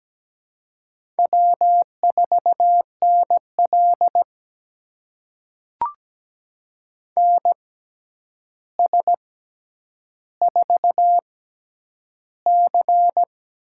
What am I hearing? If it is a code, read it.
Morse code, heard as a keyed tone, about 17 wpm, spaced as for 9 wpm, 700 Hz. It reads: W4NL NS4C
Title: USA Call Signs - 4 Characters - 6x Character Spacing - 17wpm